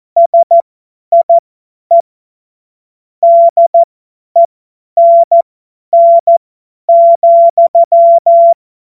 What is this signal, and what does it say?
Morse code, heard as a keyed tone, about 14 wpm, spaced as for 10 wpm, 680 Hz. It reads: SIE DENN,
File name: Morsekurs / Witze / Witz 11. Juli 2023 50 F70